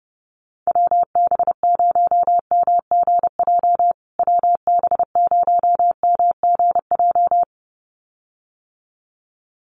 Morse code, heard as a keyed tone, about 30 wpm, 700 Hz. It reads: WB0MGJ WB0MGJ
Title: USA Call Signs Repeated 2x 30wpm